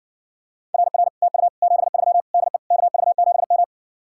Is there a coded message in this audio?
Yes: Morse code from 0.7 to 3.6 s, heard as a keyed tone, about 60 wpm, 700 Hz. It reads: RR TU 73 DE KV7M